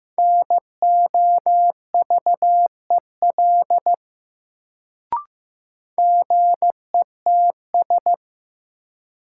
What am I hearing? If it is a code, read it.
Morse code, heard as a keyed tone, about 16 wpm, 700 Hz. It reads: NOVEL GETS
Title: Sentences from Top 1300 Words - New Word Drill - 15wpm